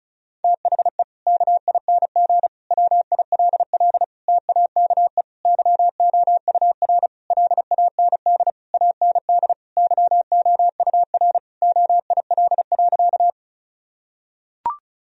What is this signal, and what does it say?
Morse code, heard as a keyed tone, about 36 wpm, 700 Hz. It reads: THE KING WILL TAKE YOUR LAND AND YOUR OIL.
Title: Sentences from Top 1000 Words - Review Part 4 - 35wpm